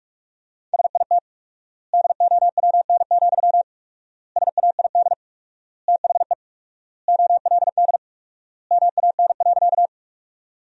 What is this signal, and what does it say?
Morse code, heard as a keyed tone, about 47 wpm, 700 Hz. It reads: SIT DOWN, SAID THE OLD MAN.
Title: 4x Word-Spacing - Sentences from Top 500 Words - 45wpm